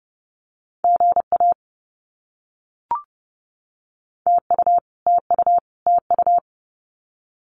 Morse code, heard as a keyed tone, about 30 wpm, 700 Hz. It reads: GA TU TU TU
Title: Abbreviations Repeated 3x - 30wpm